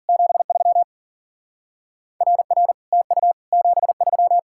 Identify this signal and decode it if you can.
Morse code, heard as a keyed tone, about 48 wpm, 700 Hz. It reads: 73 RR TU 73
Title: Speed-Racing Common QSO Phrases - 40wpm